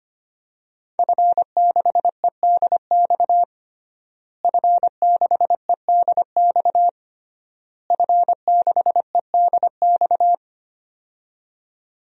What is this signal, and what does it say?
Morse code, heard as a keyed tone, about 25 wpm, 700 Hz. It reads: F6EDX F6EDX F6EDX
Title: DXCC-100 Call Signs - 5 Characters - Repeated 3x - 25wpm